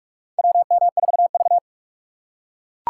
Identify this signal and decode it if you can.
Morse code, heard as a keyed tone, about 44 wpm, 700 Hz. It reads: WM4V
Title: USA Call Signs - 4 Characters - 6x Character Spacing - 45wpm